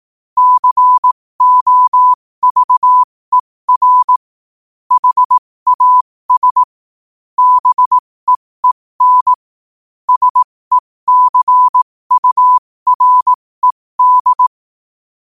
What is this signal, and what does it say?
Morse code, heard as a keyed tone, about 18 wpm, spaced as for 15 wpm, 1000 Hz. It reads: COVER HAS BEEN SECURED
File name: qotdfiles / 15wpm